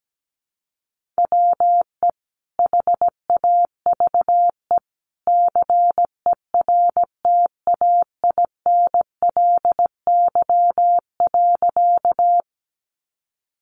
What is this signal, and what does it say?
Morse code, heard as a keyed tone, about 17 wpm, 700 Hz. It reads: WE HAVE CERTAINLY.
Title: Sentences from Top 900 Words 17wpm